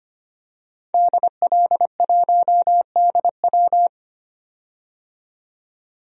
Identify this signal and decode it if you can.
Morse code, heard as a keyed tone, about 25 wpm, 700 Hz. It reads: DL1DW
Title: DXCC-100 Call Signs - 5 Characters - 25wpm